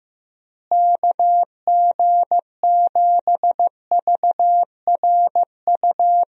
Morse code, heard as a keyed tone, about 16 wpm, 700 Hz. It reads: KG7VRU
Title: USA Call Signs - 6 Characters - Repeated 3x - 15wpm